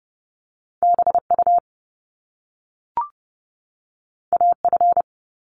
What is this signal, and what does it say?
Morse code, heard as a keyed tone, about 30 wpm, 700 Hz. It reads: BU AF
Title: Top Two Letter Combinations in Words 30wpm